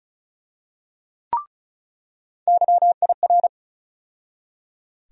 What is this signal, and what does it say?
Morse code, heard as a keyed tone, about 34 wpm, 700 Hz. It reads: YIR
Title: USA Call Sign Postfixes - 3 Letters 35wpm